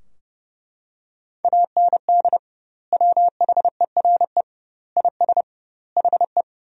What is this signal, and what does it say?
Morse code, heard as a keyed tone, about 31 wpm, 700 Hz. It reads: AND WHERE IS HE
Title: 2x Word-Spacing - Sentences from Top 200 Words - 30wpm